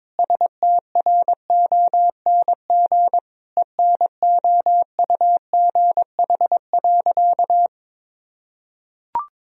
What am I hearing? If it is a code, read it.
Morse code, heard as a keyed tone, about 22 wpm, 700 Hz. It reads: STRONG ENOUGH.
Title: Sentences from Top 1000 Words - Review Part 3 - 22wpm